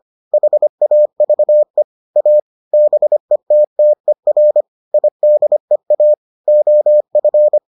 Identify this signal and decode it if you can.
Morse code, heard as a keyed tone, about 25 wpm, 600 Hz. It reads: HAVE A BETTER IDEA OF